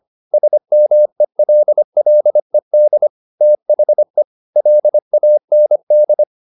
Morse code, heard as a keyed tone, about 25 wpm, 600 Hz. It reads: SMELLED THE LAND
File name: the old man and the sea part 1 25wpm